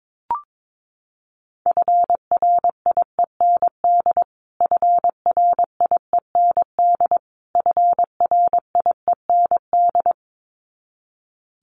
Morse code, heard as a keyed tone, about 22 wpm, 700 Hz. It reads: FRIEND FRIEND FRIEND
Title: Top 500 Words Repeated 22wpm